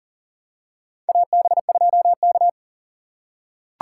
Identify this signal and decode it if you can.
Morse code, heard as a keyed tone, about 40 wpm, 700 Hz. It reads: AB2K
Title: USA Call Signs - 4 Characters - 6x Character Spacing - 40wpm